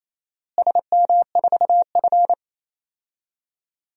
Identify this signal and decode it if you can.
Morse code, heard as a keyed tone, about 27 wpm, 700 Hz. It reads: SM4F
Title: DXCC-100 Call Signs - 4 Characters - Repeated 3x - 28wpm